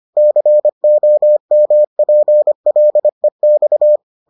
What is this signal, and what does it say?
Morse code, heard as a keyed tone, about 25 wpm, 600 Hz. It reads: COMPLEX